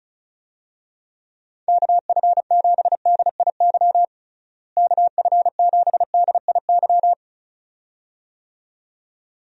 Morse code, heard as a keyed tone, about 36 wpm, 700 Hz. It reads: KF7DIY KF7DIY
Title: USA Call Signs - 6 Characters - Repeated 2x - 35wpm